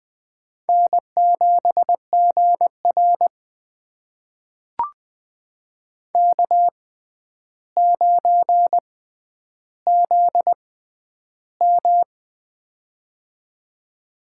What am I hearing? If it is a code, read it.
Morse code, heard as a keyed tone, about 21 wpm, spaced as for 10 wpm, 700 Hz. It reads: N7GR K9ZM
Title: USA Call Signs - 4 Characters - 6x Character Spacing - 20wpm